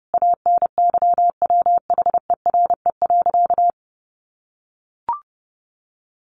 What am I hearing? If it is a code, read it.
Morse code, heard as a keyed tone, about 30 wpm, 700 Hz. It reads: ANYWHERE.
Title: Sentences from Top 1400 Words 30wpm